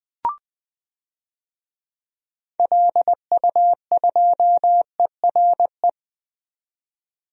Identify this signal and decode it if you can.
Morse code, heard as a keyed tone, about 20 wpm, 700 Hz. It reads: LU2ERE
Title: DXCC-100 Call Signs - 6 Characters - 20wpm